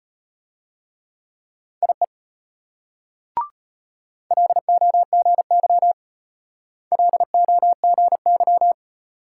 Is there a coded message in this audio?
Yes: Morse code from 1.8 to 8.7 s, heard as a keyed tone, about 32 wpm, 700 Hz. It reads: IE LOGY LOGY